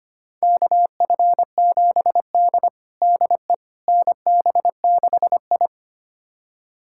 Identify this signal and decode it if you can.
Morse code, heard as a keyed tone, about 25 wpm, 700 Hz. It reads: KF7D DE NB6I
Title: USA Callsign DE Callsign - 4 Characters - 25wpm